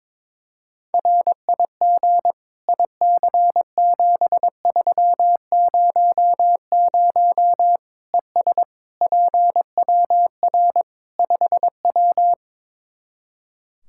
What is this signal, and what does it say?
Morse code, heard as a keyed tone, about 21 wpm, 700 Hz. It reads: RIG IC7300 ES PWR 5W
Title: Practice QSO - Exchange 2 - Info - 22wpm